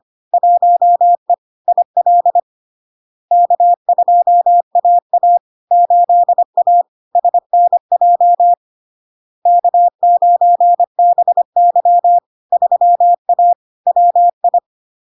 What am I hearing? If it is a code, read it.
Morse code, heard as a keyed tone, about 25 wpm, 700 Hz. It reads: 1E IL K2AA 8A SNJ K9BY 3A WI
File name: LARK-FD-2022-Log-in-CW-25wpm-700-Hz